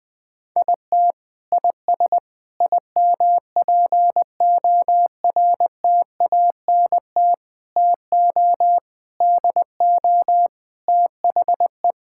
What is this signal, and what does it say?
Morse code, heard as a keyed tone, about 20 wpm, 700 Hz. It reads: IT IS IMPORTANT TO DO THE
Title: Sentences with 5-Word Phrases - 20wpm